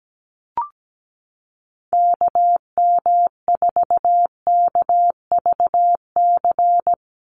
Morse code, heard as a keyed tone, about 17 wpm, 700 Hz. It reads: KM4KVC
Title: USA Call Signs 17wpm